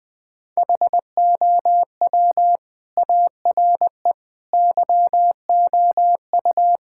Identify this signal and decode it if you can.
Morse code, heard as a keyed tone, about 20 wpm, 700 Hz. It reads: HOW ARE YOU
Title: Build A Phrase - 5-Word Phrases - 20wpm